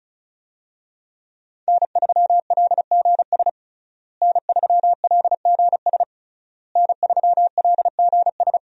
Morse code, heard as a keyed tone, about 36 wpm, 700 Hz. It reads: N3LGS N3LGS N3LGS
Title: USA Call Signs - 5 Characters - Repeated 3x - 35wpm